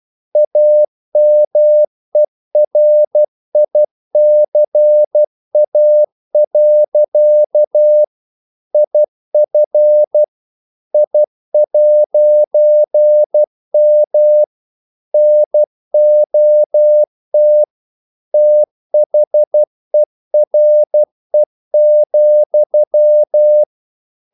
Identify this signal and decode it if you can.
Morse code, heard as a keyed tone, about 12 wpm, 600 Hz. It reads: AMERICA. IF I'M NOT THERE,
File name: CW / Practice / EnglishQuotations0312